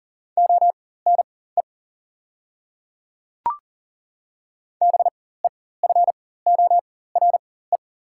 Morse code, heard as a keyed tone, about 38 wpm, spaced as for 18 wpm, 700 Hz. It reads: ONE BEFORE
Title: 4x Character Spacing - Top 200 Words - 40wpm